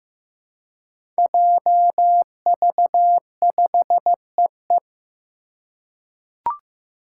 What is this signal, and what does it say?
Morse code, heard as a keyed tone, about 15 wpm, 700 Hz. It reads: JV5EE
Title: Five Challenging Letter-Number Combinations 15wpm